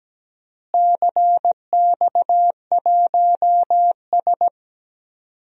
Morse code, heard as a keyed tone, about 17 wpm, 700 Hz. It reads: CX1S
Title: DXCC-100 Call Signs - 4 Characters - Repeated 3x - 17wpm